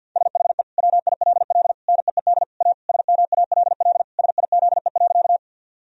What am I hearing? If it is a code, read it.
Morse code, heard as a keyed tone, about 50 wpm, 700 Hz. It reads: SHE WILL NEED A SMALL SIZE.
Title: Sentences from Top 1000 Words - Review Part 2 - 50wpm